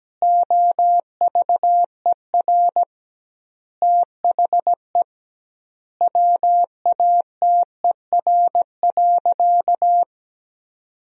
Morse code, heard as a keyed tone, about 17 wpm, 700 Hz. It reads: OVER THE WATER.
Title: 2x Word-Spacing - Sentences from Top 300 Words - 17wpm